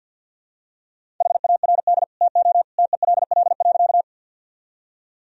Morse code, heard as a keyed tone, about 51 wpm, 700 Hz. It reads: HARD TO TELL.
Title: Sentences from Top 1000 Words - Review Part 1 - 50wpm